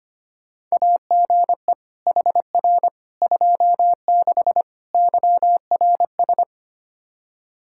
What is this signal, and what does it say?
Morse code, heard as a keyed tone, about 25 wpm, 700 Hz. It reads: AGE HR 26 YRS
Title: Practice QSO - Exchange 3 - More Info - 25wpm